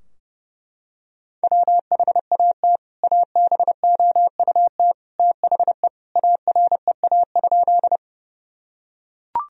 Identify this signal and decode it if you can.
Morse code, heard as a keyed tone, about 32 wpm, 700 Hz. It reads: WHAT ABOUT THE AREA?
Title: Sentences from Top 1000 Words - Review Part 2 - 30wpm